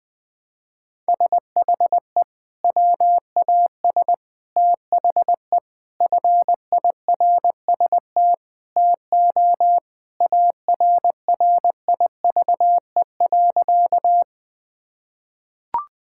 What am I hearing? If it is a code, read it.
Morse code, heard as a keyed tone, about 20 wpm, 700 Hz. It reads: SHE WAS THE FIRST TO ARRIVE.
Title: Sentences with 3-Word Phrases - Spoken Beginning - 20wpm